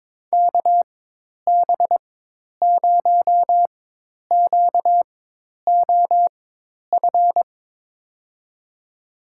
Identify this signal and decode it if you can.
Morse code, heard as a keyed tone, about 22 wpm, spaced as for 10 wpm, 700 Hz. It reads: KB0QOF
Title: USA Call Signs - 6 Characters - 4x Character Spacing - 22wpm